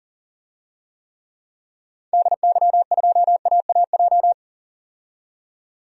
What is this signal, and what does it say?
Morse code, heard as a keyed tone, about 40 wpm, 700 Hz. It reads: DY2AAJ
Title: DXCC-100 Call Signs - 6 Characters - 40wpm